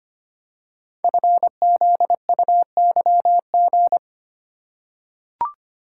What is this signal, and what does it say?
Morse code, heard as a keyed tone, about 25 wpm, 700 Hz. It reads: FZUYG
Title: Code Groups - 5 Letter:Numbers - 25wpm